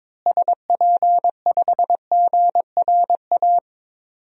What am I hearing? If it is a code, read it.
Morse code, heard as a keyed tone, about 21 wpm, 700 Hz. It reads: SP5GRA
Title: DXCC-100 Call Signs - 6 Characters - Repeated 2x - 22wpm